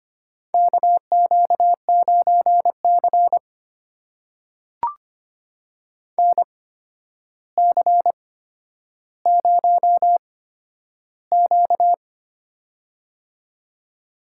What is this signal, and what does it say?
Morse code, heard as a keyed tone, about 25 wpm, spaced as for 11 wpm, 700 Hz. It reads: KQ9C NC0Q
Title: USA Call Signs - 4 Characters - 8x Character Spacing - 25wpm